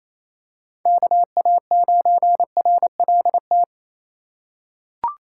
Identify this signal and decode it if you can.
Morse code, heard as a keyed tone, about 28 wpm, 700 Hz. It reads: KA9RLT
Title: USA Call Signs - 6 Characters - 2x Character Spacing - 28wpm